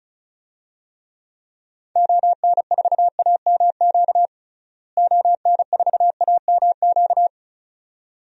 Morse code, heard as a keyed tone, about 34 wpm, 700 Hz. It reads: ON4AMQ ON4AMQ
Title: DXCC-100 Call Signs - 6 Characters - Repeated 2x - 35wpm